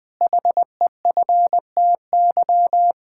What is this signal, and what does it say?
Morse code, heard as a keyed tone, about 20 wpm, 700 Hz. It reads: HEFTY
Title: Pangram Challenge v2 - 20wpm